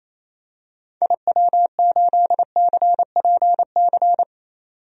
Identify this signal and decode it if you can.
Morse code, heard as a keyed tone, about 28 wpm, 700 Hz. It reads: IW8CPC
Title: DXCC-100 Call Signs - 6 Characters - 28wpm